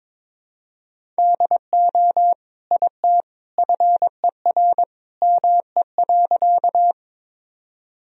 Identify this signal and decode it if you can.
Morse code, heard as a keyed tone, about 22 wpm, 700 Hz. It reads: DO IT FER ME.
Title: Sentences from Top 200 Words and Abbreviations 22wpm